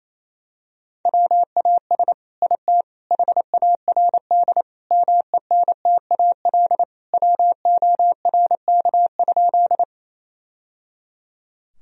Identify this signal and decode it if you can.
Morse code, heard as a keyed tone, about 28 wpm, 700 Hz. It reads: WAS IT HARD MENTAL WORK?
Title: Sentences from Top 1000 Words - Review Part 4 - 28wpm